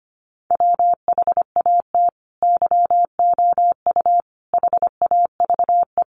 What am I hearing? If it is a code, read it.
Morse code, heard as a keyed tone, about 25 wpm, 700 Hz. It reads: WHAT YOU HAVE
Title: Sentences from Top 200 Words 25wpm